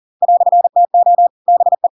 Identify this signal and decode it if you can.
Morse code, heard as a keyed tone, about 40 wpm, 700 Hz. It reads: "TO BE
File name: Pinocchio / Pinocchio ch8 40 40 ik1hge